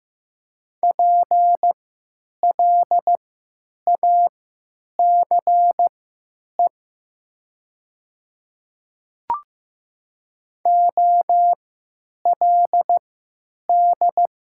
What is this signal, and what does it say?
Morse code, heard as a keyed tone, about 15 wpm, spaced as for 9 wpm, 700 Hz. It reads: PLACE OLD
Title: 3x Character Spacing - Top 200 Words - 15wpm